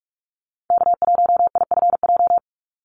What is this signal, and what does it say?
Morse code, heard as a keyed tone, about 44 wpm, 700 Hz. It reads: K1IFJ
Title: Call Signs Repeated 3x - 45wpm